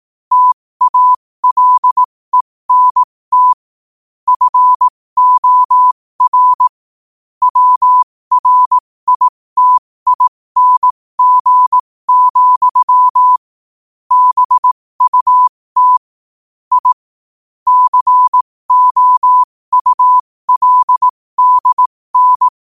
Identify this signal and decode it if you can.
Morse code, heard as a keyed tone, about 18 wpm, spaced as for 15 wpm, 1000 Hz. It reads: TALENT FOR WRITING, BUT I COULDN